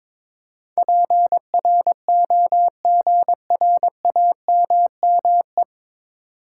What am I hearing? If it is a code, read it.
Morse code, heard as a keyed tone, about 22 wpm, 700 Hz. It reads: PROGRAMME
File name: 9 Letter Words - First 3 Letters Given - 22wpm